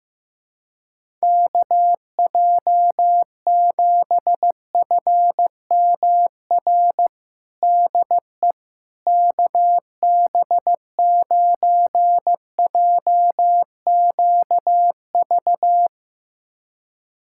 Morse code, heard as a keyed tone, about 15 wpm, 700 Hz. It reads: KJ7FMR DE KB9JQV
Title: USA Callsign DE Callsign 15wpm